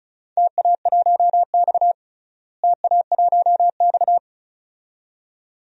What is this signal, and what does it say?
Morse code, heard as a keyed tone, about 34 wpm, 700 Hz. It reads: TA1X TA1X
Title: DXCC-100 Call Signs - 4 Characters - Repeated 2x - 35wpm